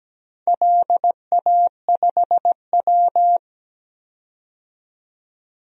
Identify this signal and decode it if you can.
Morse code, heard as a keyed tone, about 17 wpm, 700 Hz. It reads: LA5W
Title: DXCC-100 Call Signs - 4 Characters - 17wpm